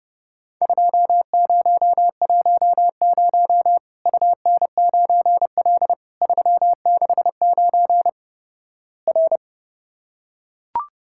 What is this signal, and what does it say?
Morse code, heard as a keyed tone, about 31 wpm, 700 Hz. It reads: 2010 UN9L 369 R
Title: Worked All Europe DX Contest - QTC Exchange - 30wpm